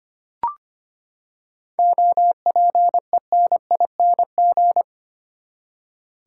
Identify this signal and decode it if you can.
Morse code, heard as a keyed tone, about 25 wpm, 700 Hz. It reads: OPENING
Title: Seven Letter Words - 25wpm